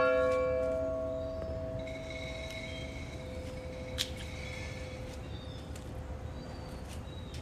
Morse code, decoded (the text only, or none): none